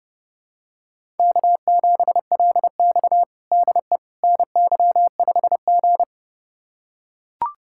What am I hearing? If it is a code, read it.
Morse code, heard as a keyed tone, about 30 wpm, 700 Hz. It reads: K7LX DE NY5G
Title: USA Callsign DE Callsign - 4 Characters - 30wpm